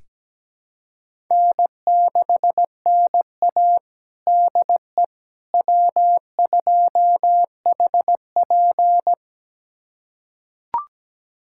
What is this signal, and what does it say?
Morse code, heard as a keyed tone, about 17 wpm, 700 Hz. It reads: N6NA DE W2HP
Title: USA Callsign DE Callsign - 4 Characters - 17wpm